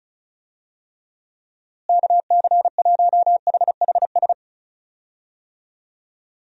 Morse code, heard as a keyed tone, about 34 wpm, 700 Hz. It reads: KC1HHS
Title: USA Call Signs - 6 Characters - Repeated 3x - 35wpm